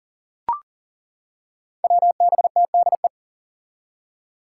Morse code, heard as a keyed tone, about 40 wpm, 700 Hz. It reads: WBTDE
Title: Five Challenging Letter-Number Combinations 40wpm